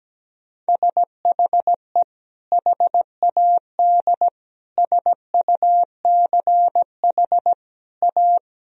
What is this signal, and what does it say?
Morse code, heard as a keyed tone, about 17 wpm, 700 Hz. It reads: SHE HAD SUCH A
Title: Sentences from Top 1000 Words - Review Part 1 - 17wpm